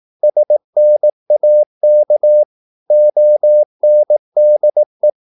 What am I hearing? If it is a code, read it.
Morse code, heard as a keyed tone, about 18 wpm, 600 Hz. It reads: SNAK ONDE
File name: Korte ord DK 18wpm